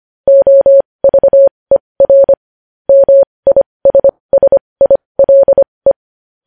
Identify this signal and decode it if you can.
Morse code, heard as a keyed tone, about 25 wpm, spaced as for 20 wpm, 550 Hz. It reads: OVER MISSILE
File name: podcast / News headlines-1x-25x20